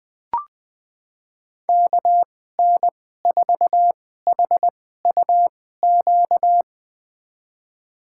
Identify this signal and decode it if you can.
Morse code, heard as a keyed tone, about 20 wpm, spaced as for 14 wpm, 700 Hz. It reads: KN4HUQ
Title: USA Call Signs - 6 Characters - 2x Character Spacing - 20wpm